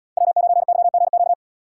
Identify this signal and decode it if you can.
Morse code, heard as a keyed tone, about 70 wpm, 700 Hz. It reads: W9YGQ